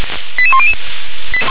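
Five-tone selective call (ZVEI), heard as quick runs of five tones, tones of about 70 ms. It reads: 88100
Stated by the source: ZVEI five-tone selective call received by SDR.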